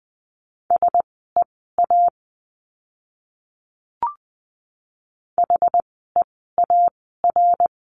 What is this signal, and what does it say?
Morse code, heard as a keyed tone, about 20 wpm, spaced as for 14 wpm, 700 Hz. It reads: SEA HEAR
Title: 061 Lesson 11f - Letter h - Words 20wpm